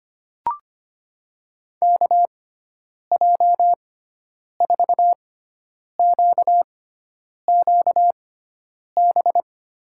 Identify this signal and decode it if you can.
Morse code, heard as a keyed tone, about 25 wpm, spaced as for 9 wpm, 700 Hz. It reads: KJ4QQB
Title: USA Call Signs - 6x Character Spacing - 25wpm